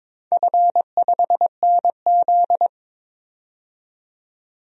Morse code, heard as a keyed tone, about 22 wpm, 700 Hz. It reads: F5NZ
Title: DXCC-100 Call Signs - 4 Characters - 22wpm